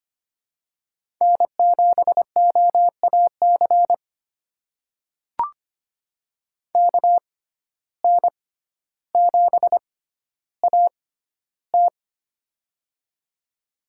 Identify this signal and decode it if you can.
Morse code, heard as a keyed tone, about 25 wpm, spaced as for 13 wpm, 700 Hz. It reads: N7OAC KN7AT
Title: USA Call Signs - 5 Characters - 6x Character Spacing - 25wpm